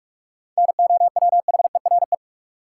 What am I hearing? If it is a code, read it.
Morse code, heard as a keyed tone, about 41 wpm, 700 Hz. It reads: NOWHERE
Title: Sets of 5 Words Encore - New Word Drill - 45wpm